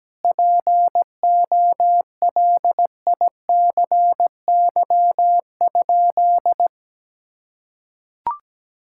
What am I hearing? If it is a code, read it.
Morse code, heard as a keyed tone, about 17 wpm, 700 Hz. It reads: POLICY?
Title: Sentences from Top 1000 Words - Review Part 5 - 17wpm